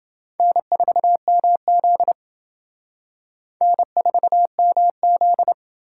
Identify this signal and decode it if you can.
Morse code, heard as a keyed tone, about 28 wpm, 700 Hz. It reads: N4MZ N4MZ